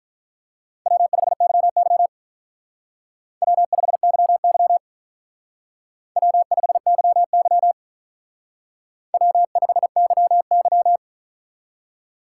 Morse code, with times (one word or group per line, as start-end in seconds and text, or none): 0.9-2.1 W5YY
3.4-4.8 W5YY
6.2-7.7 W5YY
9.1-11.0 W5YY